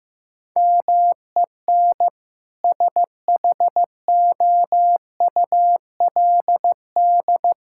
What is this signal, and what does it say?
Morse code, heard as a keyed tone, about 15 wpm, 700 Hz. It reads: MEN SHOULD
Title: Sentences from Top 1600 Words 15wpm